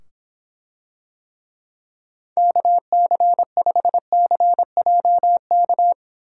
Morse code, heard as a keyed tone, about 24 wpm, 700 Hz. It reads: KC5CJK